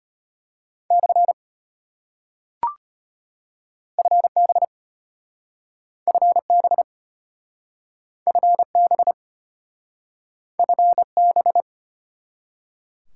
Morse code, heard as a keyed tone, about 33 wpm, 700 Hz. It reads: / FB FB FB FB